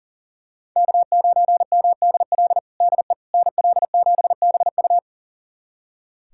Morse code, heard as a keyed tone, about 37 wpm, 700 Hz. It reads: K9MDL DE NL7BU
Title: USA Callsign DE Callsign - 5 Characters - 40wpm